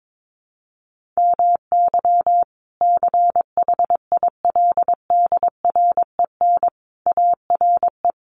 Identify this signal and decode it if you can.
Morse code, heard as a keyed tone, about 22 wpm, 700 Hz. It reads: MY CHILDREN ARE
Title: Sentences from Top 1500 Words 22wpm